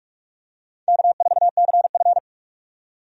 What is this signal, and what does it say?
Morse code, heard as a keyed tone, about 46 wpm, 700 Hz. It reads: K4CF
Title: USA Call Signs - 4 Characters - 45wpm